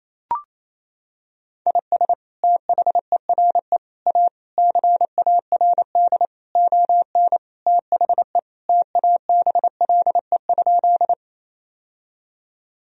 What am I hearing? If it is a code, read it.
Morse code, heard as a keyed tone, about 27 wpm, 700 Hz. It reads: IS THERE A CARD ON THE TABLE?
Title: Sentences from Top 1000 Words - Review Part 4 - 28wpm